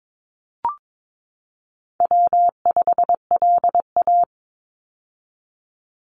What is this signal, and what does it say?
Morse code, heard as a keyed tone, about 22 wpm, 700 Hz. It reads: W5LA
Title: Call Signs - 22wpm